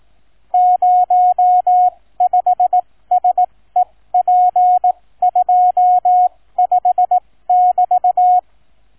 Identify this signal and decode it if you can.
Morse code, heard as a keyed tone, about 18 wpm, spaced as for 15 wpm, 730 Hz. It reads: 05SEP25<BT>